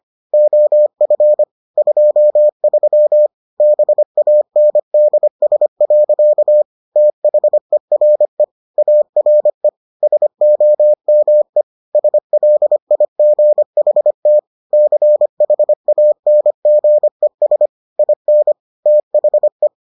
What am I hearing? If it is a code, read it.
Morse code, heard as a keyed tone, about 25 wpm, 600 Hz. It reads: OF 23 BANDS. THERE ARE SOME SLIGHT CHANGES IN THE